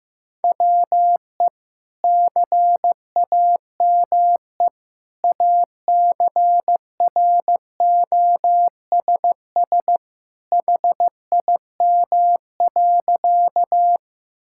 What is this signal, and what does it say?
Morse code, heard as a keyed tone, about 15 wpm, 700 Hz. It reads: WE CAME ACROSS HIM.
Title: Sentences from Top 1000 Words - Review Part 5 - 15wpm